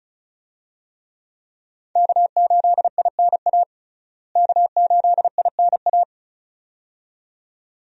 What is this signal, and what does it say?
Morse code, heard as a keyed tone, about 34 wpm, 700 Hz. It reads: K8INA K8INA
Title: USA Call Signs - 5 Characters - Repeated 2x - 35wpm